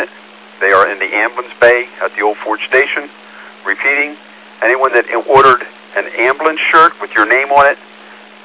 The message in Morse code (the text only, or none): none